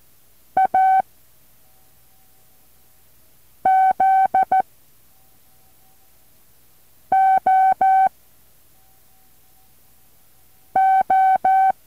Morse code, heard as a keyed tone, about 14 wpm, 760 Hz. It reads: A Z O O
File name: intl morse code2